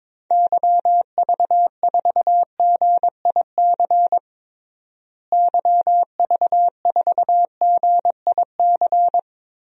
Morse code, heard as a keyed tone, about 22 wpm, 700 Hz. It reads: YV4GIC YV4GIC
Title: DXCC-100 Call Signs - 6 Characters - Repeated 3x - 22wpm